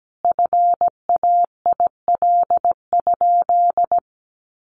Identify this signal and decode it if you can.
Morse code, heard as a keyed tone, about 17 wpm, 700 Hz. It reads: FAIL?
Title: Sentences from Top 1400 Words 17wpm